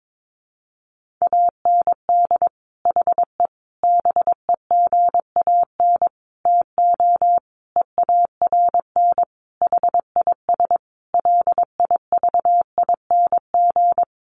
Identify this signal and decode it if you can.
Morse code, heard as a keyed tone, about 22 wpm, 700 Hz. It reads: AND HE BEGAN TO EARN HIS LIVING
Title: Sentences from Top 1500 Words 22wpm